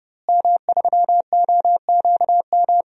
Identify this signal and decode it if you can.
Morse code, heard as a keyed tone, about 30 wpm, 700 Hz. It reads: M3OQM
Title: DXCC-100 Call Signs - 5 Characters - Repeated 3x - 30wpm